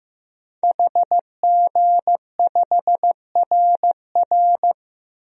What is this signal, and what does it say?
Morse code, heard as a keyed tone, about 15 wpm, 700 Hz. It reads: HG5RR
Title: DXCC-100 Call Signs - 5 Characters - 15wpm